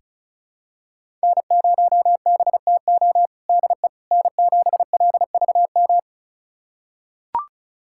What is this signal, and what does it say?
Morse code, heard as a keyed tone, about 36 wpm, 700 Hz. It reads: N0BTO DE N7LVM
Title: USA Callsign DE Callsign - 5 Characters - 35wpm